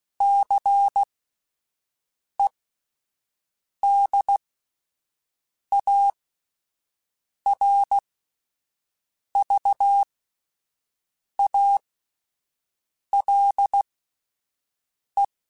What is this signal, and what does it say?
Morse code, heard as a keyed tone, about 16 wpm, spaced as for 5 wpm, 800 Hz. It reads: CEDARVALE
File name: fromSMs / FromSMs mp3s / Cities@5